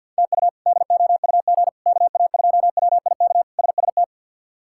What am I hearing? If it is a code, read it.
Morse code, heard as a keyed tone, about 53 wpm, 700 Hz. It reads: TU DOUG KA2WIK SST
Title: K1USN SST QSO Practice - 50wpm